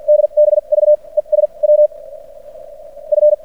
Morse code, heard as a keyed tone, about 53 wpm, spaced as for 18 wpm, 600 Hz. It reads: DBV ES WF
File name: archived / CRSnet